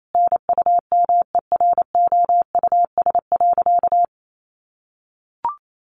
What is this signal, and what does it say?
Morse code, heard as a keyed tone, about 28 wpm, 700 Hz. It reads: NUMEROUS.
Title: Sentences from Top 1400 Words 28wpm